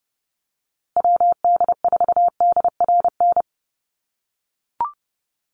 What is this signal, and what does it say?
Morse code, heard as a keyed tone, about 30 wpm, 700 Hz. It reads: WD4DRN
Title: USA Call Signs Repeated 2x 30wpm